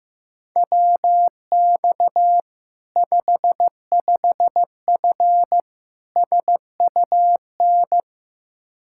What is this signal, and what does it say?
Morse code, heard as a keyed tone, about 15 wpm, 700 Hz. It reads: WX 55F SUN
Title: Practice QSO - Exchange 2 - Info - 15wpm